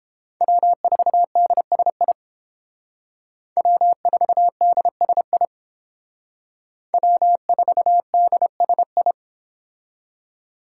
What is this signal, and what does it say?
Morse code, heard as a keyed tone, about 31 wpm, 700 Hz. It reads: W4DSI W4DSI W4DSI